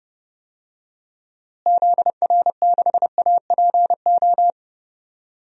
Morse code, heard as a keyed tone, about 30 wpm, 700 Hz. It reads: ZR6APO
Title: DXCC-100 Call Signs - 6 Characters - Repeated 3x - 30wpm